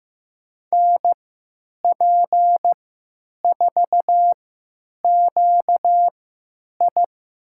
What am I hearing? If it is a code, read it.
Morse code, heard as a keyed tone, about 15 wpm, spaced as for 9 wpm, 700 Hz. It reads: NP4QI
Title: USA Call Signs - 5 Characters - 3x Character Spacing - 15wpm